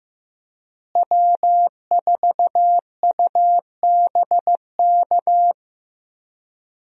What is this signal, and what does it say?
Morse code, heard as a keyed tone, about 15 wpm, 700 Hz. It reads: W4UBK
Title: USA Call Signs - 5 Characters - Repeated 3x - 15wpm